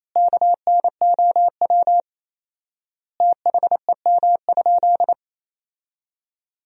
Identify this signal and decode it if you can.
Morse code, heard as a keyed tone, about 28 wpm, 700 Hz. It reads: KNOW THEM?
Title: 4x Word-Spacing - Sentences from Top 100 Words - 28wpm